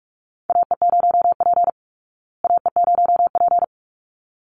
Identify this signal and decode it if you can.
Morse code, heard as a keyed tone, about 44 wpm, 700 Hz. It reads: AE0P AE0P
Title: Call Signs Repeated 3x - 45wpm